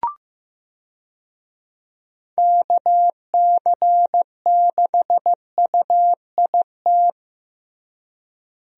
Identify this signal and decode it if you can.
Morse code, heard as a keyed tone, about 15 wpm, 700 Hz. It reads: KC6UIT
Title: USA Call Signs - 6 Characters - Repeated 2x - 15wpm